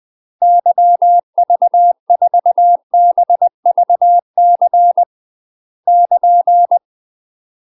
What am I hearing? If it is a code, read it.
Morse code, heard as a keyed tone, about 20 wpm, 700 Hz. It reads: YV4BVC <KN>
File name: UK MASTER QSOs TRAINING / 20WPM / UK MASTER QSOs 0061